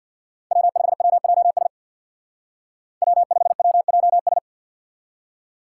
Morse code, heard as a keyed tone, about 59 wpm, 700 Hz. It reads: W5WJS W5WJS
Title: Speed Racing - USA 1x3 Call Signs 50wpm